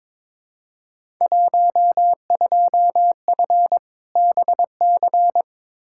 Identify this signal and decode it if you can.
Morse code, heard as a keyed tone, about 21 wpm, 700 Hz. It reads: 12F BC
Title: Field Day Exchanges Encore - 22wpm